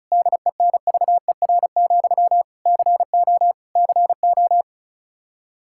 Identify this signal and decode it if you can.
Morse code, heard as a keyed tone, about 34 wpm, 700 Hz. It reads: DENVER, CO CO
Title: Practice QSO - Exchange 1 - Beginning - 35wpm